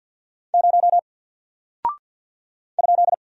Morse code, heard as a keyed tone, about 48 wpm, 700 Hz. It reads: 0 ?
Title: Single Character (Letter:Number:Punctuation:Prosign) 50wpm